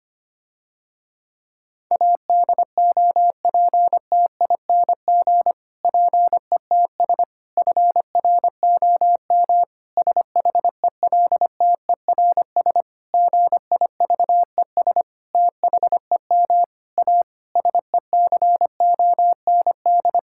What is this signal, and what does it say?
Morse code, heard as a keyed tone, about 25 wpm, 700 Hz. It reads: ADOPTING PETS FROM SHELTERS GIVES THEM A SECOND
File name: practice / 25wpm / Difficult Sentences with Hints - 25wpm